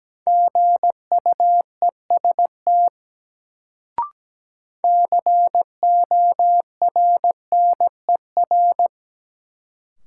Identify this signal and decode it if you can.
Morse code, heard as a keyed tone, about 17 wpm, 700 Hz. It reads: GUEST CORNER
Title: Sentences from Top 1600 Words - New Word Drill - 17wpm